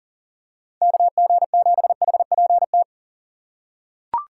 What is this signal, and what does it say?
Morse code, heard as a keyed tone, about 40 wpm, 700 Hz. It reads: KG7HPT
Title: USA Call Signs - 6x Character Spacing - 40wpm